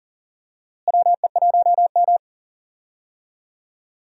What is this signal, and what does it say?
Morse code, heard as a keyed tone, about 37 wpm, 700 Hz. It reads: WE1M
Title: USA Call Signs - 4 Characters - Repeated 3x - 40wpm